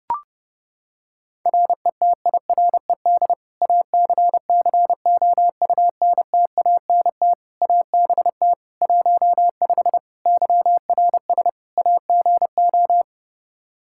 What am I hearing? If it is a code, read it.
Morse code, heard as a keyed tone, about 32 wpm, 700 Hz. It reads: RETIRED ACCOUNTANT ABT 15 YRS AGO
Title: Practice QSO - Exchange 3 - More Info - 30wpm